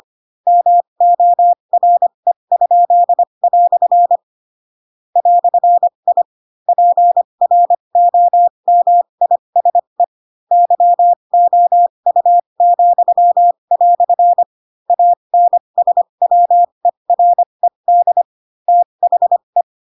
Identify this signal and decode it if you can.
Morse code, heard as a keyed tone, about 25 wpm, spaced as for 22 wpm, 700 Hz. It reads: MORE?" "I PROMISE YOU," ANSWERED THE
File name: Pinocchio / Pinocchio ch8 18 25 ik1hge